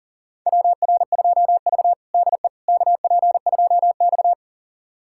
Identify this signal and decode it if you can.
Morse code, heard as a keyed tone, about 40 wpm, 700 Hz. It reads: WR2V DE KP2X
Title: USA Callsign DE Callsign - 4 Characters - 40wpm